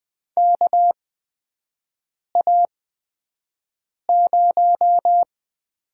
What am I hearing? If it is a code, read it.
Morse code, heard as a keyed tone, about 20 wpm, spaced as for 5 wpm, 700 Hz. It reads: KA0
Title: USA Call Signs - 6 Characters - 8x Character Spacing - 20wpm